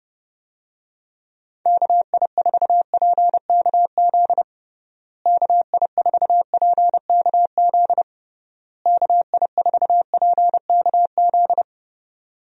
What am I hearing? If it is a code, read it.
Morse code, heard as a keyed tone, about 30 wpm, 700 Hz. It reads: KI4PKZ KI4PKZ KI4PKZ
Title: USA Call Signs - 6 Characters - Repeated 3x - 30wpm